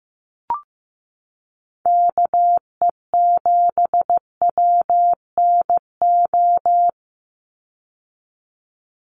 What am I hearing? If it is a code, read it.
Morse code, heard as a keyed tone, about 15 wpm, 700 Hz. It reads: KE7WNO
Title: USA Call Signs 15wpm